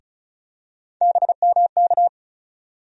Morse code, heard as a keyed tone, about 34 wpm, 700 Hz. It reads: BMK
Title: USA Call Sign Postfixes - 3 Letters 35wpm